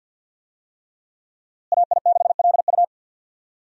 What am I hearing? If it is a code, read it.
Morse code, heard as a keyed tone, about 51 wpm, 690 Hz. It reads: AI6LU
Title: USA Call Signs - 5 Characters - Repeated 3x - 50wpm